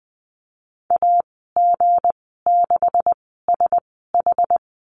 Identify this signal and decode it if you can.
Morse code, heard as a keyed tone, about 19 wpm, spaced as for 14 wpm, 700 Hz. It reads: AG6SH
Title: 167 Lesson 27g - Number 7 - Call Signs 20wpm